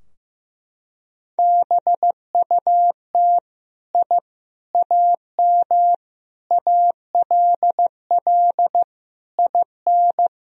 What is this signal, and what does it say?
Morse code, heard as a keyed tone, about 15 wpm, 700 Hz. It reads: BUT I AM ALL IN
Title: Sentences from Top 1000 Words - Review Part 2 - 15wpm